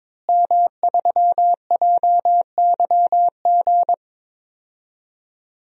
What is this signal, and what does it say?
Morse code, heard as a keyed tone, about 22 wpm, 700 Hz. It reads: M3JYG
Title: DXCC-100 Call Signs - 5 Characters - Repeated 3x - 22wpm